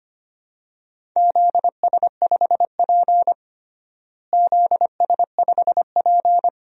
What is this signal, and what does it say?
Morse code, heard as a keyed tone, about 25 wpm, 700 Hz. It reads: ZS5P ZS5P
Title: DXCC-100 Call Signs - 4 Characters - Repeated 2x - 25wpm